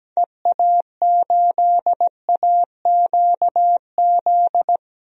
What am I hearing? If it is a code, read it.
Morse code, heard as a keyed tone, about 17 wpm, 700 Hz. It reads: EA8AQZ
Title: DXCC-100 Call Signs - 6 Characters - Repeated 2x - 17wpm